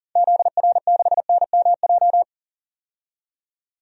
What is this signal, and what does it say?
Morse code, heard as a keyed tone, about 40 wpm, 700 Hz. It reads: ZR6NMJ
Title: DXCC-100 Call Signs - 6 Characters - 40wpm